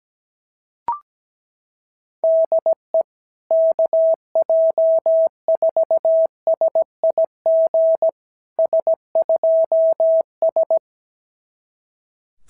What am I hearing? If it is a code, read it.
Morse code, heard as a keyed tone, about 16 wpm, 650 Hz. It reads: DE KJ4SIG S2S
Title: SOTA QSO Practice 17wpm